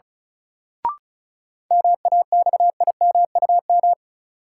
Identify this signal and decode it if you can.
Morse code, heard as a keyed tone, about 36 wpm, 700 Hz. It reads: MAXIMUM
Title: Sentences from Top 1300 Words - New Word Drill Alt - 35wpm